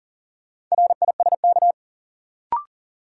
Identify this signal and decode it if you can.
Morse code, heard as a keyed tone, about 40 wpm, 700 Hz. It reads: RISK
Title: Sentences from Top 600 Words - New Word Drill - 40wpm